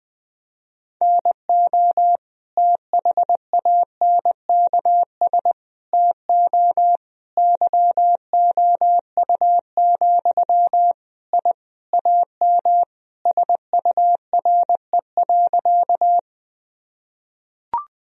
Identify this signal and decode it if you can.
Morse code, heard as a keyed tone, about 20 wpm, 700 Hz. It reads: NO THANKS TO YOU, I AM SURE.
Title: Sentences from Top 1000 Words - Review Part 3 - 20wpm